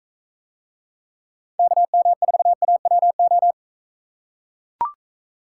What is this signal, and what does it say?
Morse code, heard as a keyed tone, about 42 wpm, 700 Hz. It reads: KM4AWO